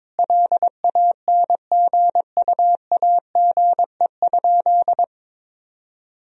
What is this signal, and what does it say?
Morse code, heard as a keyed tone, about 22 wpm, 700 Hz. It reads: LANGUAGE?
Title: Sentences from Top 1000 Words - Review Part 3 - 22wpm